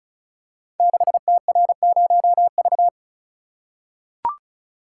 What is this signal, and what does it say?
Morse code, heard as a keyed tone, about 36 wpm, 690 Hz. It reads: 6TR0V
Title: Code Groups - 5 Letter:Numbers - 35wpm